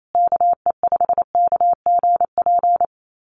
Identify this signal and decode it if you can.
Morse code, heard as a keyed tone, about 28 wpm, 700 Hz. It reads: KE5KGP
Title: USA Call Signs Repeated 3x 28wpm